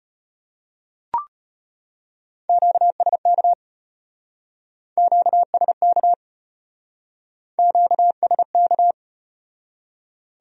Morse code, heard as a keyed tone, about 35 wpm, 700 Hz. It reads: QSK QSK QSK